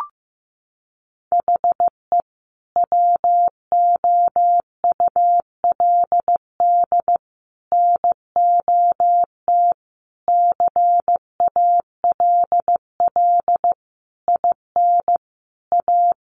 Morse code, heard as a keyed tone, about 15 wpm, 700 Hz. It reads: HE WOULD NOT CALL IN A
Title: Sentences from Top 1500 Words 15wpm